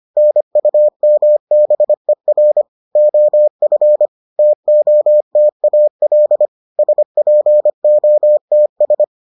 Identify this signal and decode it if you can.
Morse code, heard as a keyed tone, about 25 wpm, 600 Hz. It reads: NUMBER OF TOTAL SPOTS